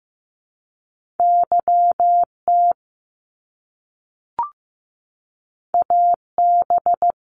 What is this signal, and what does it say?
Morse code, heard as a keyed tone, about 15 wpm, 700 Hz. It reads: YT AB
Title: Canadian Provinces and Territories - Repeated 3x - 15wpm